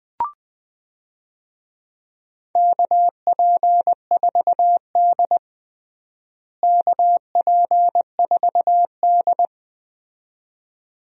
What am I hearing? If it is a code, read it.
Morse code, heard as a keyed tone, about 20 wpm, 700 Hz. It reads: KP4D KP4D
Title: DXCC-100 Call Signs - 4 Characters - Repeated 2x - 20wpm